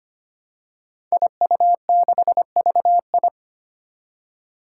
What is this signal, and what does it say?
Morse code, heard as a keyed tone, about 25 wpm, 700 Hz. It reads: IU6VI
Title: Five Challenging Letter-Number Combinations 25wpm